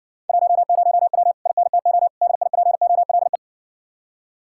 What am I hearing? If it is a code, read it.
Morse code, heard as a keyed tone, about 60 wpm, 700 Hz. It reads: 10W INTO DIPOLE